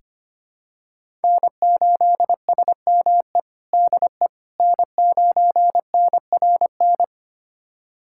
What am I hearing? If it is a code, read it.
Morse code, heard as a keyed tone, about 25 wpm, 700 Hz. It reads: N8SME DE N9NRN
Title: USA Callsign DE Callsign - 5 Characters - 25wpm